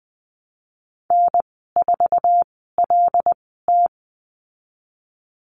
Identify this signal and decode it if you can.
Morse code, heard as a keyed tone, about 20 wpm, spaced as for 15 wpm, 700 Hz. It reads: N4LT
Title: 088 Lesson 15g - Number 5 - Call Signs 20wpm